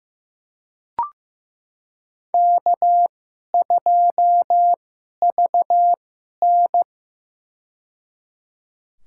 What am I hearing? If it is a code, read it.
Morse code, heard as a keyed tone, about 15 wpm, spaced as for 11 wpm, 700 Hz. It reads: K2VN
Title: USA Call Signs - 4 Characters - 2x Character Spacing - 15wpm